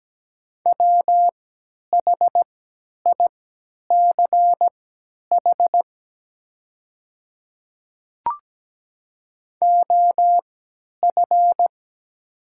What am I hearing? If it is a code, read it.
Morse code, heard as a keyed tone, about 17 wpm, spaced as for 10 wpm, 700 Hz. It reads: WHICH OF
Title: 3x Character Spacing - Top 100 Words - 17wpm